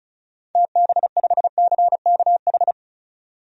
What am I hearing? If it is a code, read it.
Morse code, heard as a keyed tone, about 35 wpm, 700 Hz. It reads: TB5CKH
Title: DXCC-100 Call Signs - 6 Characters - 35wpm